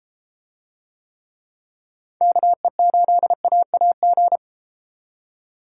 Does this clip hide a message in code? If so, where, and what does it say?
Yes: Morse code from 2.2 to 4.4 s, heard as a keyed tone, about 32 wpm, 700 Hz. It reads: KE8AAG